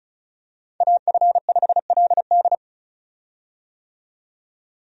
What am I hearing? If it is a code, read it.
Morse code, heard as a keyed tone, about 36 wpm, 700 Hz. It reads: AF5LD
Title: USA Call Signs - 5 Characters - 35wpm